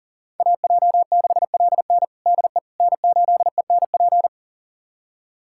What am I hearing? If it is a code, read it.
Morse code, heard as a keyed tone, about 40 wpm, 700 Hz. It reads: AJ6LN DE N8ENP
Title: USA Callsign DE Callsign - 5 Characters - 40wpm